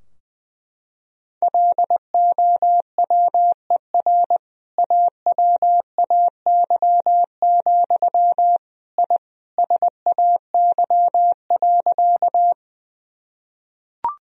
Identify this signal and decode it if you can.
Morse code, heard as a keyed tone, about 20 wpm, 700 Hz. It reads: LOWER AWAY, I SAY.
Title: Sentences from Top 1000 Words - Review Part 4 - 20wpm